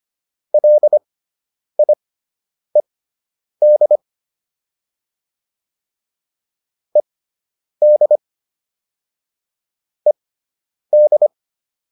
Morse code, heard as a keyed tone, about 25 wpm, spaced as for 9 wpm, 600 Hz. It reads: LIED ED ED